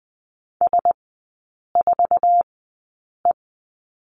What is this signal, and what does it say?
Morse code, heard as a keyed tone, about 20 wpm, spaced as for 8 wpm, 700 Hz. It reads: S4E
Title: 047 Lesson 9e - Number 4 - Three Characters 20wpm